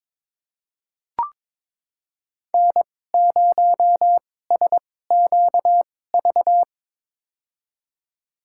Morse code, heard as a keyed tone, about 22 wpm, spaced as for 16 wpm, 700 Hz. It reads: N0SQV
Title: USA Call Signs - 5 Characters - 2x Character Spacing - 22wpm